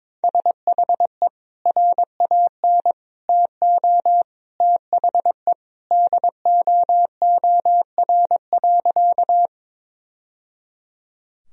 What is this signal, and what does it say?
Morse code, heard as a keyed tone, about 22 wpm, 700 Hz. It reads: SHE RAN TO THE DOOR.
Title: Sentences from Top 1000 Words - Review Part 3 - 22wpm